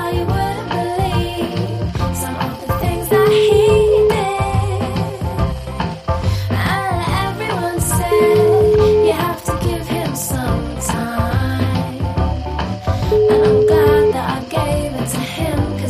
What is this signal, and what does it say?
Ringback tone (Europe, 425 Hz), with noise.